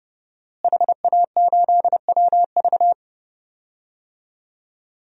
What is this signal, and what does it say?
Morse code, heard as a keyed tone, about 30 wpm, 700 Hz. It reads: HA8WV
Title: DXCC-100 Call Signs - 5 Characters - Repeated 2x - 30wpm